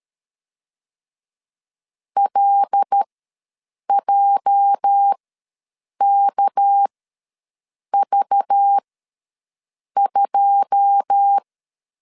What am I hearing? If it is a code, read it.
Morse code, heard as a keyed tone, about 13 wpm, spaced as for 6 wpm, 800 Hz. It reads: LJKV2
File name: LEZIONI / 10 SIGNS / 10B INT ws 12wpm 3527